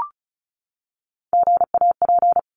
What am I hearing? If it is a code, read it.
Morse code, heard as a keyed tone, about 34 wpm, 700 Hz. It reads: GAP
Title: Top 1501-1750 Words 35wpm